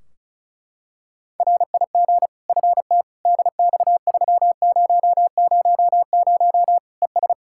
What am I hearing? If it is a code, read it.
Morse code, heard as a keyed tone, about 36 wpm, 700 Hz. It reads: RIG FT DX3000 ES
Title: Practice QSO - Exchange 2 - Info - 35wpm